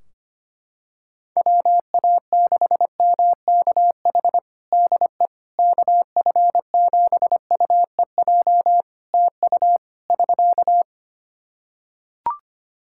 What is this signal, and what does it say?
Morse code, heard as a keyed tone, about 25 wpm, 700 Hz. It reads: WA6MKH DE KF7UEJ TU <SK>
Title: K1USN SST QSO Practice 25wpm